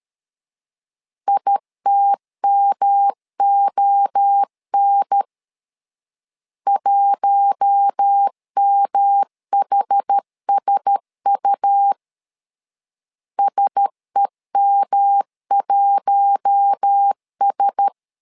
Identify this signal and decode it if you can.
Morse code, heard as a keyed tone, about 13 wpm, 800 Hz. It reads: ITMON 1MHSU SEM1S
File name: LEZIONI / 4 Q1S 12wpm 1802